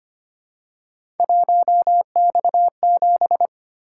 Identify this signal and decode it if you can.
Morse code, heard as a keyed tone, about 25 wpm, 700 Hz. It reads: 1X7
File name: Three Letter-Number Combinations - 25wpm